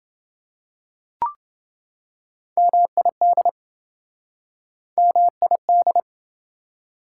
Morse code, heard as a keyed tone, about 26 wpm, 700 Hz. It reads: MID MID